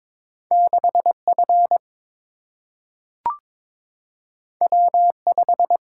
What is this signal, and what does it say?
Morse code, heard as a keyed tone, about 22 wpm, 700 Hz. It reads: 6 F W 5
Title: Single-Letter Single-Number 22wpm